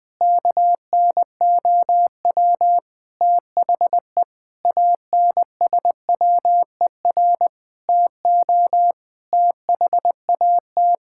Morse code, heard as a keyed tone, about 20 wpm, 700 Hz. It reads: KNOW THE ANSWER TO THAT
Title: Sentences with 7-Word Phrases - Spoken Beginning - 20wpm